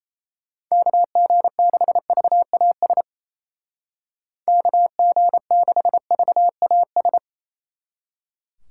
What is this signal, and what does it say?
Morse code, heard as a keyed tone, about 35 wpm, 700 Hz. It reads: KG6VAS KG6VAS